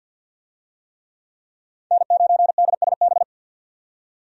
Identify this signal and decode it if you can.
Morse code, heard as a keyed tone, about 52 wpm, 700 Hz. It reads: N9DSB
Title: USA Call Signs - 5 Characters - 50wpm